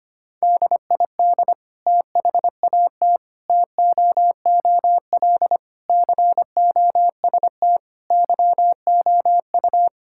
Morse code, heard as a keyed tone, about 25 wpm, 700 Hz. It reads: DID THAT TOOL COST YOU
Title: Sentences from Top 1000 Words - Review Part 6 - 25wpm